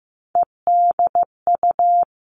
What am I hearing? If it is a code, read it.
Morse code, heard as a keyed tone, about 15 wpm, 700 Hz. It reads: EDU
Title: Three Challenging Letter-Number Combinations 15wpm